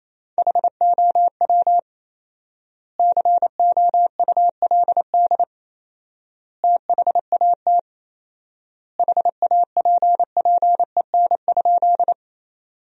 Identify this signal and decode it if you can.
Morse code, heard as a keyed tone, about 28 wpm, 700 Hz. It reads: HOW COULD THAT HAPPEN?
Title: 4x Word-Spacing - Sentences from Top 400 Words - 28wpm